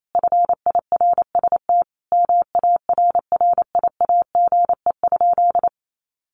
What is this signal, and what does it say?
Morse code, heard as a keyed tone, about 28 wpm, 700 Hz. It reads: FIRST MARRIAGE?
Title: Sentences from Top 1500 Words 28wpm